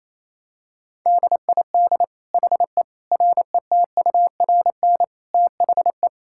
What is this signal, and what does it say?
Morse code, heard as a keyed tone, about 28 wpm, 700 Hz. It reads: DID HE RETURN THE
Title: Sentences from Top 1000 Words - Review Part 5 - 28wpm